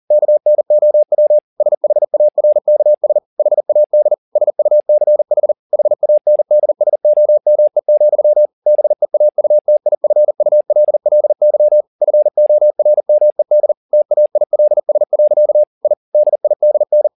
Morse code, heard as a keyed tone, about 40 wpm, 600 Hz. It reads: KNOW SHARKS HAD SUCH HANDSOME, BEAUTIFULLY FORMED TAILS. I DIDN
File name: the old man and the sea 2 40wpm